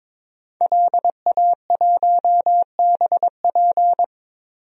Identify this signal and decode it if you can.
Morse code, heard as a keyed tone, about 21 wpm, 700 Hz. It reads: LA1BP
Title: DXCC-100 Call Signs - 5 Characters - Repeated 3x - 22wpm